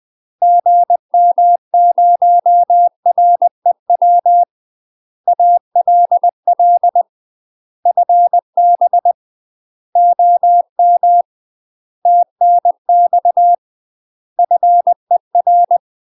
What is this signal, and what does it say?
Morse code, heard as a keyed tone, about 20 wpm, 700 Hz. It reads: GM0REW ALL FB OM TNX FER
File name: UK MASTER QSOs TRAINING / 20WPM / UK MASTER QSOs 0072